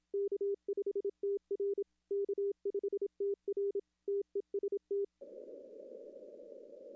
Morse code, heard as a keyed tone, about 27 wpm, 400 Hz. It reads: K5TR K5TR TEST
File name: contests / arrl-160-k5tr-op-w5tn -0600